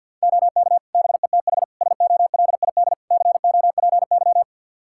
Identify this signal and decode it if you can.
Morse code, heard as a keyed tone, about 51 wpm, 700 Hz. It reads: OK BETH SOLID COPY
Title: Practice QSO - Exchange 3 - More Info - 50wpm